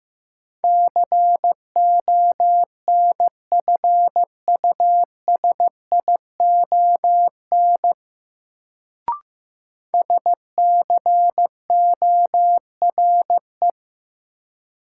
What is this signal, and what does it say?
Morse code, heard as a keyed tone, about 15 wpm, 700 Hz. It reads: CONFUSION SCORE
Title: Sentences from Top 1600 Words - New Word Drill - 15wpm